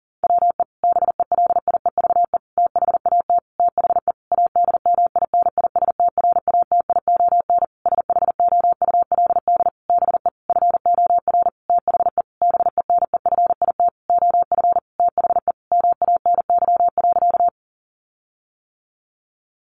Morse code, heard as a keyed tone, about 40 wpm, 700 Hz. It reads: WE BELIEVE THAT THE ADMINISTRATION SHOULD BE FOR THE BENEFIT OF THE MANY.
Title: Sentences from Top 1300 Words 40wpm